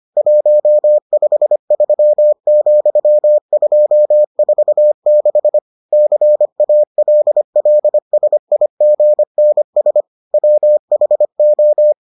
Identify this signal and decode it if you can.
Morse code, heard as a keyed tone, about 25 wpm, 600 Hz. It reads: 153,246 CALLSIGNS WHO